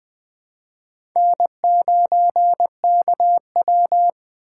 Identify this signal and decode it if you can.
Morse code, heard as a keyed tone, about 20 wpm, 700 Hz. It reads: N9KW
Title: 2023 SOTA Chaser Call Signs - 20wpm